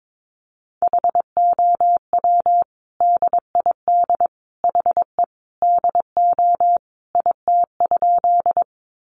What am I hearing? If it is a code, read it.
Morse code, heard as a keyed tone, about 22 wpm, 700 Hz. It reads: HOW DID HE DO IT?
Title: Sentences from Top 100 Words 22wpm